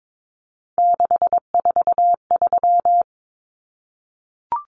Morse code, heard as a keyed tone, about 22 wpm, 700 Hz. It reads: 643
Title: Common Cut Numbers - 3-Digit 22wpm